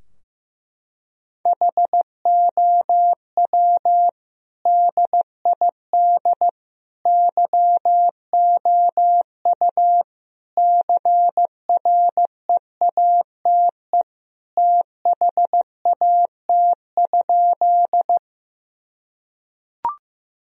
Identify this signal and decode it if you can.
Morse code, heard as a keyed tone, about 15 wpm, 700 Hz. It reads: HOW DID YOU CREATE THAT?
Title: Sentences from Top 1000 Words - Review Part 3 - 15wpm